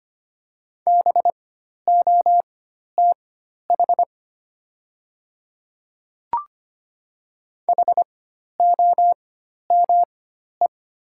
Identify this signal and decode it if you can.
Morse code, heard as a keyed tone, about 25 wpm, spaced as for 12 wpm, 690 Hz. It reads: BOTH HOME
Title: 4x Character Spacing - Top 500 Words - 25wpm